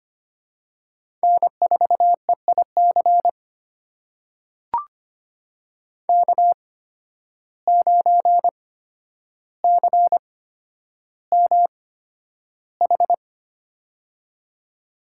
Morse code, heard as a keyed tone, about 25 wpm, spaced as for 11 wpm, 700 Hz. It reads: N4EIC K9CMH
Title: USA Call Signs - 8x Character Spacing - 25wpm